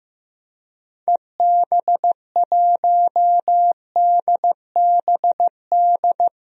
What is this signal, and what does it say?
Morse code, heard as a keyed tone, about 15 wpm, 700 Hz. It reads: EB1DBD
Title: DXCC-100 Call Signs - 6 Characters - Repeated 3x - 15wpm